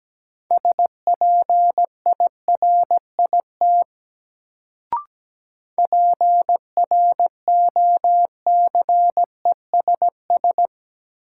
Morse code, heard as a keyed tone, about 17 wpm, 700 Hz. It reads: SPIRIT PROCESS
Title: Sentences from Top 1400 Words - New Word Drill Alt - 17wpm